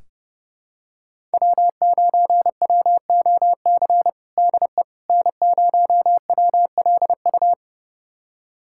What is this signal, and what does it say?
Morse code, heard as a keyed tone, about 30 wpm, 700 Hz. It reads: W9WOC DE N0WLU
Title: USA Callsign DE Callsign - 5 Characters - 30wpm